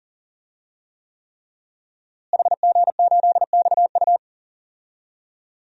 Morse code, heard as a keyed tone, about 38 wpm, 700 Hz. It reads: HG8XU
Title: DXCC-100 Call Signs - 5 Characters - Repeated 3x - 40wpm